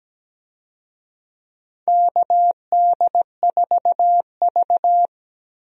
Morse code, heard as a keyed tone, about 17 wpm, 700 Hz. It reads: KD4V
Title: USA Call Signs - 4 Characters - 17wpm